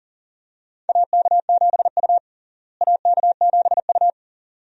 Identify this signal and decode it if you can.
Morse code, heard as a keyed tone, about 40 wpm, 700 Hz. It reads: AK7U AK7U
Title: USA Call Signs - 4 Characters - Repeated 2x - 40wpm